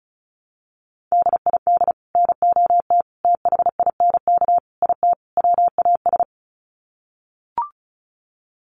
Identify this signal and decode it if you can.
Morse code, heard as a keyed tone, about 34 wpm, 700 Hz. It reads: DID NOT THINK IT WAS
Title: Sets of 5 Words Encore 35wpm